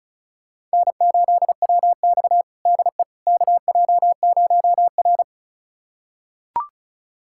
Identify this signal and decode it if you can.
Morse code, heard as a keyed tone, about 36 wpm, 700 Hz. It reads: N8WX DE KJ0R
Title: USA Callsign DE Callsign - 4 Characters - 35wpm